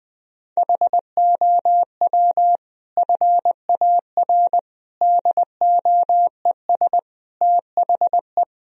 Morse code, heard as a keyed tone, about 20 wpm, 700 Hz. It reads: HOW FAR DOES THE
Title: Sentences from Top 1000 Words - Review Part 4 - 20wpm